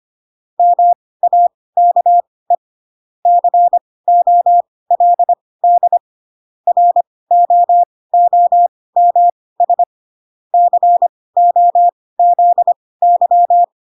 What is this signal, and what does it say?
Morse code, heard as a keyed tone, about 25 wpm, spaced as for 18 wpm, 700 Hz. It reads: MAKE COLD ROOMS COZY
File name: Pinocchio / Pinocchio ch1 12 25 ik1hge